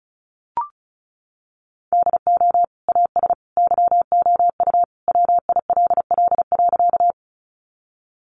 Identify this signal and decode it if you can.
Morse code, heard as a keyed tone, about 34 wpm, 700 Hz. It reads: DO AS YOU WILL.
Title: Sentences from Top 100 Words 35wpm